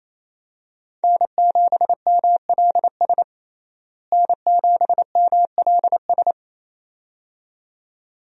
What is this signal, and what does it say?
Morse code, heard as a keyed tone, about 28 wpm, 700 Hz. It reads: N7MLS N7MLS
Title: USA Call Signs - 5 Characters - Repeated 2x - 28wpm